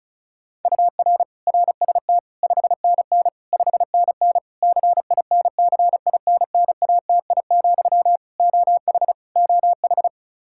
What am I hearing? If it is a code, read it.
Morse code, heard as a keyed tone, about 36 wpm, 700 Hz. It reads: UR RST 5NN 5NN CINCINNATI, OH OH
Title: Practice QSO - Exchange 1 - Beginning - 35wpm